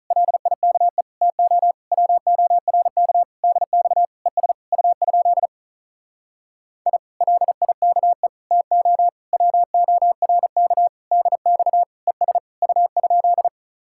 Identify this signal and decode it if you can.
Morse code, heard as a keyed tone, about 46 wpm, 700 Hz. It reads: LIKE TO WORK DX ES U? I LIKE TO WORK DX ES U?